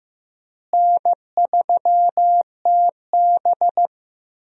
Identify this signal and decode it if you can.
Morse code, heard as a keyed tone, about 15 wpm, 700 Hz. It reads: N3TB
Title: USA Call Signs - 4 Characters - Repeated 3x - 15wpm